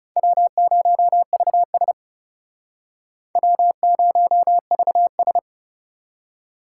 Morse code, heard as a keyed tone, about 37 wpm, 700 Hz. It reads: W0VS W0VS
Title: Speed Racing - USA 1x2 Call Signs 30wpm